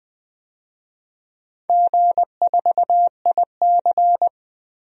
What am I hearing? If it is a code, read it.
Morse code, heard as a keyed tone, about 20 wpm, 700 Hz. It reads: G4IC
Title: DXCC-100 Call Signs - 4 Characters - 20wpm